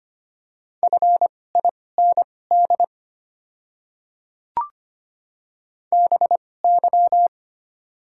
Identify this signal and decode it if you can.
Morse code, heard as a keyed tone, about 25 wpm, spaced as for 18 wpm, 700 Hz. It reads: FIND BY
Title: 2x Character Spacing - Top 100 Words - 25wpm